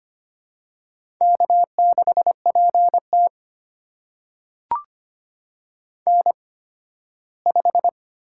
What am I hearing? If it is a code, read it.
Morse code, heard as a keyed tone, about 25 wpm, spaced as for 15 wpm, 700 Hz. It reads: K6PT N5
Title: USA Call Signs - 8x Character Spacing - 25wpm